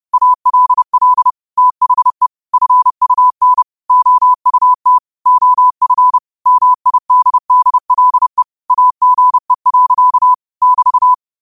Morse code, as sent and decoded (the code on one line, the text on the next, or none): .- .-.. .-.. / - .... / ..-. ..- -. / --- ..- - / --- ..-. / -- .. -.. -.. .-.. / .- --. .-.-.- / -...-
ALL THE FUN OUT OF MIDDLE AGE. <BT>